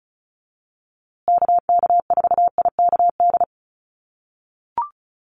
Morse code, heard as a keyed tone, about 34 wpm, 700 Hz. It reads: KK4IKD
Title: USA Call Signs Repeated 2x 35wpm